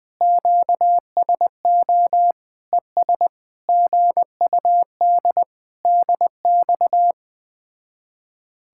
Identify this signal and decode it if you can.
Morse code, heard as a keyed tone, about 20 wpm, 700 Hz. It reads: QSO ES GUD DX
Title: Speed-Racing Common QSO Phrases Encore - 17wpm